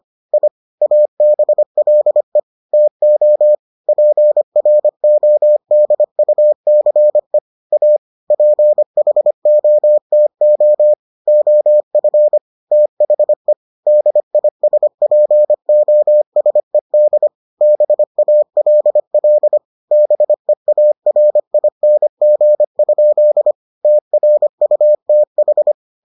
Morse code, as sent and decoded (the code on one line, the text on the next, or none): .. / .- -... .-.. / - --- / .--. .-. --- -.. ..- -.-. / .- / .--. .... --- - --- / --- ..-. / - .... / -.. .. ... .--. --- ... -.. / -... .- .-.. .-.. / -... .- .-. .. -. --. ..--.. / - .-. ..- - ....
I ABLE TO PRODUCE A PHOTO OF THE DISPOSED BALL BEARING? TRUTH